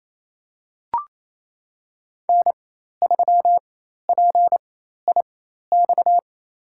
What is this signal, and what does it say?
Morse code, heard as a keyed tone, about 28 wpm, spaced as for 13 wpm, 700 Hz. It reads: N3PIX
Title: USA Call Signs - 5 Characters - 4x Character Spacing - 28wpm